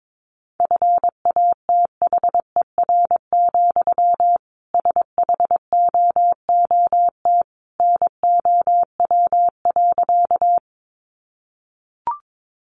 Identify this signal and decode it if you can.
Morse code, heard as a keyed tone, about 22 wpm, 700 Hz. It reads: FATHER, SHOOT NOW.
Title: Sentences from Top 1200 Words 22wpm